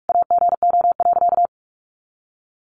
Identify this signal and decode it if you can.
Morse code, heard as a keyed tone, about 44 wpm, 700 Hz. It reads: AGO.
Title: Sentences from Top 500 Words 45wpm